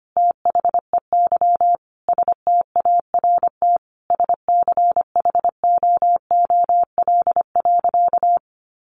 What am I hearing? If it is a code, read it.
Morse code, heard as a keyed tone, about 25 wpm, 700 Hz. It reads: THEY START SCHOOL.
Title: Sentences from Top 1100 Words 25wpm